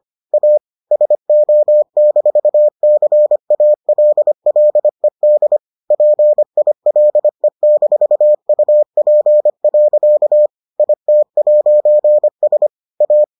Morse code, heard as a keyed tone, about 25 wpm, 600 Hz. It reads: A SO-CALLED PILE-UP. IT'S A